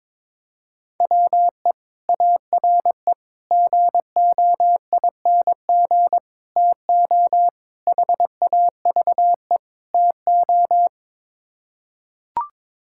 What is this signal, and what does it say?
Morse code, heard as a keyed tone, about 22 wpm, 700 Hz. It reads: WE ARE GOING TO HAVE TO
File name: practice / 22wpm / Sets of 6 Words - 22wpm